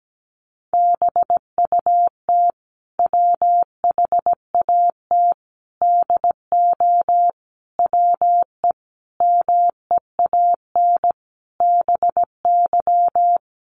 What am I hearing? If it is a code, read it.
Morse code, heard as a keyed tone, about 17 wpm, 700 Hz. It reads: BUT WHAT DO WE MEAN BY
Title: Sentences from Top 1100 Words 17wpm